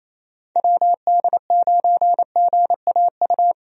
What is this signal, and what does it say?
Morse code, heard as a keyed tone, about 28 wpm, 700 Hz. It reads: WD9GAU
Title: USA Call Signs - 6 Characters - 28wpm